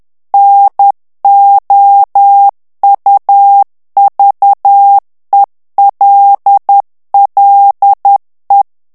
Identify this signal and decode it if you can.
Morse code, heard as a keyed tone, about 10 wpm, 800 Hz. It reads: NOUVELLE
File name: FRANCAIS TEXTE CLAIR 2 VITESSE 10 MOTS MINUTES